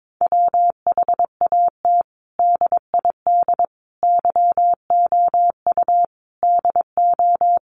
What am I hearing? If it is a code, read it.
Morse code, heard as a keyed tone, about 22 wpm, 700 Hz. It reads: WHAT DID YOU DO
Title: Sentences from Top 100 Words 22wpm